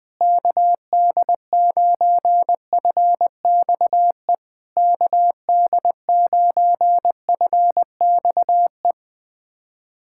Morse code, heard as a keyed tone, about 21 wpm, 690 Hz. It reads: KD9FXE KD9FXE
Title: SOTA QSO Practice 20wpm